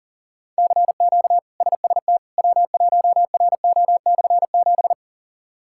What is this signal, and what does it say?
Morse code, heard as a keyed tone, about 40 wpm, 700 Hz. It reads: CQ SST W1RO/7
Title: K1USN SST QSO Practice - 40wpm